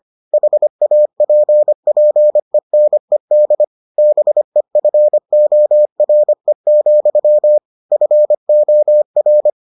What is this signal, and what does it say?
Morse code, heard as a keyed tone, about 25 wpm, 600 Hz. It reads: HAPPENED BEFORE, FOR